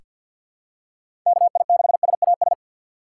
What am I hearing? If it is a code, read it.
Morse code, heard as a keyed tone, about 48 wpm, 700 Hz. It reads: K I 6 S A S
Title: USA Call Signs - 6 Characters - 6x Character Spacing - 50wpm